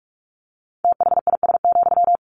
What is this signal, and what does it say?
Morse code, heard as a keyed tone, about 44 wpm, 700 Hz. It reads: THIS,
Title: Sentences from Top 1400 Words 45wpm